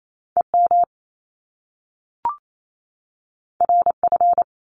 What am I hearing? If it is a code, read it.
Morse code, heard as a keyed tone, about 28 wpm, 700 Hz. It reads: E M R F
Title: Top Two Letter Combinations in Words 28wpm